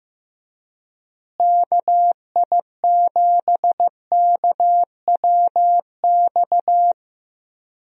Morse code, heard as a keyed tone, about 15 wpm, 700 Hz. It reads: KI7KWX
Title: USA Call Signs - 6 Characters - Repeated 3x - 15wpm